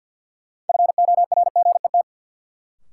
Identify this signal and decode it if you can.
Morse code, heard as a keyed tone, about 47 wpm, 700 Hz. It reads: FORGET
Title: Sentences from Top 1200 Words - New Word Drill - 50wpm